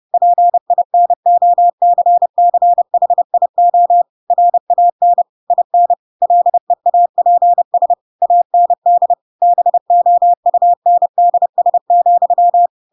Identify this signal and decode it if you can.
Morse code, heard as a keyed tone, about 30 wpm, 700 Hz. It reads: PINOCCHIO RAN IN LEAPS AND BOUNDS,
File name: Pinocchio / Pinocchio ch3 30 30 ik1hge